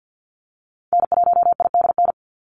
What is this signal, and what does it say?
Morse code, heard as a keyed tone, about 50 wpm, 700 Hz. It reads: N1IDN
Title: NA SOTA Activator Callsigns 50wpm